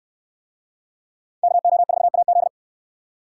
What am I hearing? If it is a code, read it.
Morse code, heard as a keyed tone, about 69 wpm, 700 Hz. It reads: KO4AZ